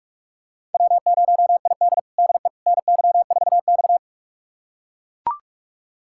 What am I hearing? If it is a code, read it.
Morse code, heard as a keyed tone, about 41 wpm, 700 Hz. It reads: W0ID DE NY4X
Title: USA Callsign DE Callsign - 4 Characters - 45wpm